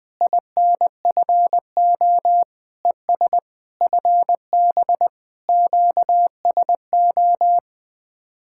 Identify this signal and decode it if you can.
Morse code, heard as a keyed tone, about 20 wpm, 700 Hz. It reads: INFO ES FB QSO
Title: Speed-Racing Common QSO Phrases - 17wpm